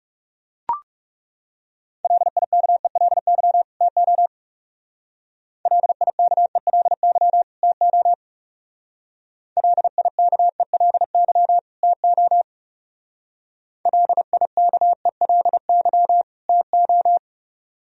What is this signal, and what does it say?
Morse code, heard as a keyed tone, about 36 wpm, 700 Hz. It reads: LIKELY TO LIKELY TO LIKELY TO LIKELY TO
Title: Speed-Racing - Sets of 2 Words 30wpm